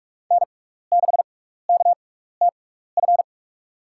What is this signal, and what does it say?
Morse code, heard as a keyed tone, about 46 wpm, spaced as for 15 wpm, 690 Hz. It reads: N6KTF
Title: USA Call Signs - 5 Characters - 6x Character Spacing - 45wpm